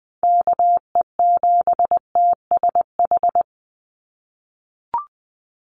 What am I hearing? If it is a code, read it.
Morse code, heard as a keyed tone, about 20 wpm, 700 Hz. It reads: KE7TSH
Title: USA Call Signs 20wpm